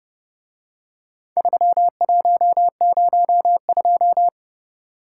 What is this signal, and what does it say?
Morse code, heard as a keyed tone, about 29 wpm, 700 Hz. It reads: 3102